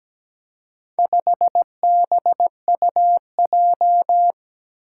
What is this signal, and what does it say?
Morse code, heard as a keyed tone, about 17 wpm, 700 Hz. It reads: 5BUJ
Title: Four Challenging Letter-Number Combinations 17wpm